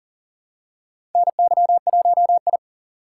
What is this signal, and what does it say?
Morse code, heard as a keyed tone, about 38 wpm, 700 Hz. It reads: NY1I
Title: USA Call Signs - 4 Characters - 2x Character Spacing - 40wpm